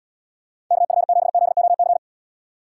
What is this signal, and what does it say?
Morse code, heard as a keyed tone, about 79 wpm, 700 Hz. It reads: KF7ZOC